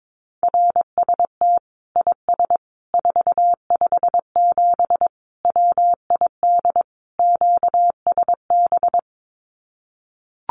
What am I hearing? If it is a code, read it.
Morse code, heard as a keyed tone, about 22 wpm, 700 Hz. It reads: RST IS 457 WID QSB
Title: Common QSO Phrases 22wpm